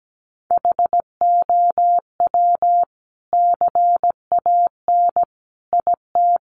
Morse code, heard as a keyed tone, about 17 wpm, 700 Hz. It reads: HOW CAN IT
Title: Sentences from Top 1100 Words 17wpm